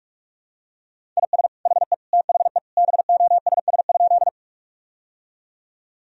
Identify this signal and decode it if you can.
Morse code, heard as a keyed tone, about 46 wpm, 700 Hz. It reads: IS HE THE BOSS?
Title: Sentences from Top 1000 Words - Review Part 4 - 45wpm